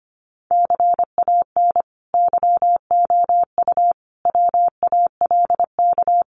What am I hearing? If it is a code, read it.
Morse code, heard as a keyed tone, about 25 wpm, 700 Hz. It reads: CAN YOU WALK
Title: Sentences from Top 1000 Words 25wpm